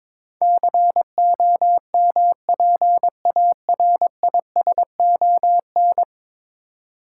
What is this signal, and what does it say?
Morse code, heard as a keyed tone, about 22 wpm, 700 Hz. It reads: COMPARISON
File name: practice / 22wpm / 10 Letter Words - First 3 Letters Given - 22wpm